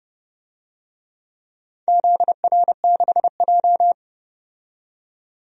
Morse code, heard as a keyed tone, about 30 wpm, 700 Hz. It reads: ZR6J
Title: DXCC-100 Call Signs - 4 Characters - 30wpm